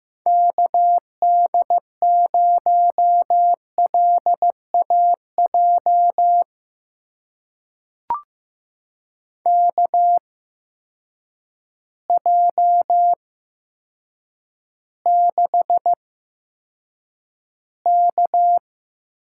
Morse code, heard as a keyed tone, about 15 wpm, spaced as for 8 wpm, 700 Hz. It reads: KD0LAJ KJ6K
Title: USA Call Signs - 6 Characters - 8x Character Spacing - 15wpm